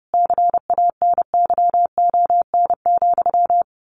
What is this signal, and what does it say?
Morse code, heard as a keyed tone, about 30 wpm, 700 Hz. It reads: CANYON,
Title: A Sound Of Thunder 30wpm